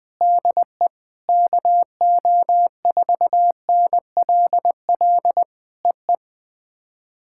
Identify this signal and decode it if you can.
Morse code, heard as a keyed tone, about 20 wpm, 700 Hz. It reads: DE KO4NLL EE
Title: POTA QSO Practice - 20wpm